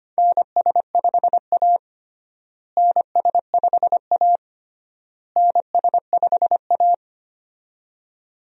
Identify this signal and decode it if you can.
Morse code, heard as a keyed tone, about 25 wpm, 700 Hz. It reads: NS5A NS5A NS5A
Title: USA Call Signs - 4 Characters - Repeated 3x - 25wpm